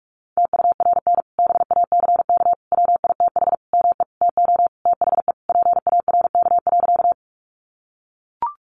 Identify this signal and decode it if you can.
Morse code, heard as a keyed tone, about 44 wpm, 700 Hz. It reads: TURN BACK WITH ME TO THE PARK.
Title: Sentences from Top 1300 Words 45wpm